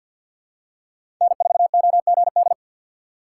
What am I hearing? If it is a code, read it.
Morse code, heard as a keyed tone, about 53 wpm, 690 Hz. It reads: N4OGD
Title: USA Call Signs - 5 Characters - 2x Character Spacing - 50wpm